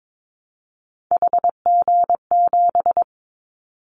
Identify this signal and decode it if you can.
Morse code, heard as a keyed tone, about 22 wpm, 700 Hz. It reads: HG7
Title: Callsign Prefixes for Top 50 DXCC - 22wpm